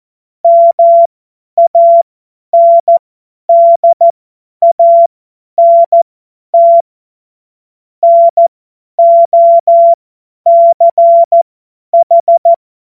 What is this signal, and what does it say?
Morse code, heard as a keyed tone, about 14 wpm, spaced as for 10 wpm, 680 Hz. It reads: MANDANT NOCH
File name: Morsekurs / Witze / Witz 23. August 2022 50 F70